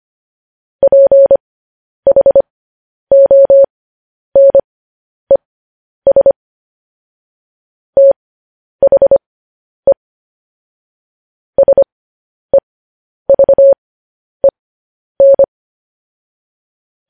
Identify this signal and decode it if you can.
Morse code, heard as a keyed tone, about 25 wpm, spaced as for 10 wpm, 550 Hz. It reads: PHONES THE SEVEN
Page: News Headlines in Morse Code at 10 WPM